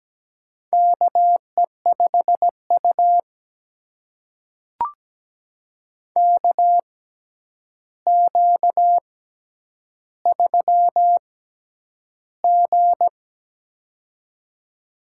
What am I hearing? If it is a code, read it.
Morse code, heard as a keyed tone, about 17 wpm, spaced as for 9 wpm, 700 Hz. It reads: KE5U KQ3G
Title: USA Call Signs - 4 Characters - 6x Character Spacing - 17wpm